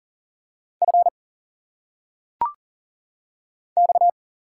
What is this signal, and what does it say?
Morse code, heard as a keyed tone, about 38 wpm, 700 Hz. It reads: F X
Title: Single Letters - Phonetic Pronunciation 40wpm